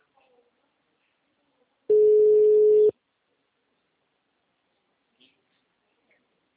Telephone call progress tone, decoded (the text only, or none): ringback tone